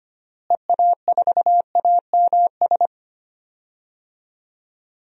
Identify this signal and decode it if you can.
Morse code, heard as a keyed tone, about 25 wpm, 700 Hz. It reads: EA4AMS
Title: DXCC-100 Call Signs - 6 Characters - Repeated 3x - 25wpm